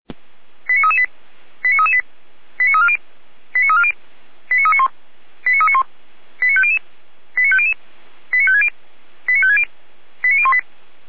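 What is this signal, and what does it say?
Five-tone selective call (ZVEI), heard as quick runs of five tones, tones of about 70 ms. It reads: 89338 89338 89340 89340 89391 89391 89500 89500 89560 89560 89018
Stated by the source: Five-tone ZVEI selective call received on SDR, from the feuerwehr-orlamuende.org fire brigade audio archive.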